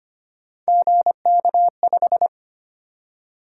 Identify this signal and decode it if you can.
Morse code, heard as a keyed tone, about 25 wpm, 700 Hz. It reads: GK5
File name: Three Letter-Number Combinations - 25wpm